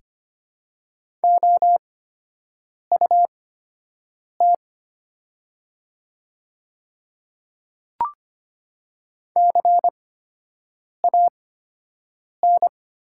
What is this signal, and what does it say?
Morse code, heard as a keyed tone, about 25 wpm, spaced as for 7 wpm, 700 Hz. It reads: OUT CAN
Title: 8x Character Spacing - Top 200 Words - 25wpm